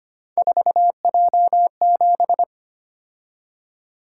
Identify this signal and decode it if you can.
Morse code, heard as a keyed tone, about 25 wpm, 700 Hz. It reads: 4J7
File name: Three Letter-Number Combinations - 25wpm